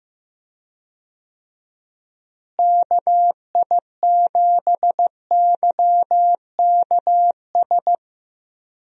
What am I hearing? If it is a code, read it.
Morse code, heard as a keyed tone, about 15 wpm, 700 Hz. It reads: KI7YKS
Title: USA Call Signs - 6 Characters - Repeated 3x - 15wpm